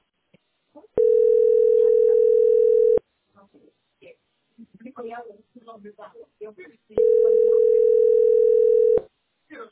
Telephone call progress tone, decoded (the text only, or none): ringback tone